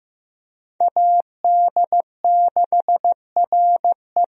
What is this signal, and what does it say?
Morse code, heard as a keyed tone, about 15 wpm, 700 Hz. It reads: AD6RE
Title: USA Call Signs - 5 Characters - Repeated 3x - 15wpm